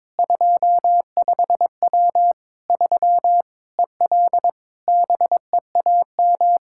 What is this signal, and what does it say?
Morse code, heard as a keyed tone, about 21 wpm, 700 Hz. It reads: 25W 3 EL BEAM
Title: Practice QSO - Exchange 2 - Info - 22wpm